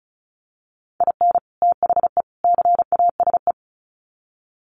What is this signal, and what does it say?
Morse code, heard as a keyed tone, about 34 wpm, 700 Hz. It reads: IN THE CASE
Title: Sets of 3 Words 35wpm